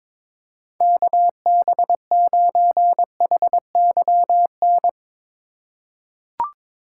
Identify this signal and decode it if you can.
Morse code, heard as a keyed tone, about 22 wpm, 700 Hz. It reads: KB9HYN
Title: USA Call Signs - 4x Character Spacing - 22wpm